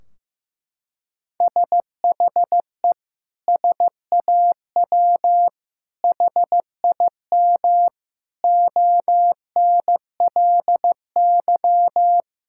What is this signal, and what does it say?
Morse code, heard as a keyed tone, about 15 wpm, 700 Hz. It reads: SHE SAW HIM ONLY
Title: Sentences from Top 1600 Words 15wpm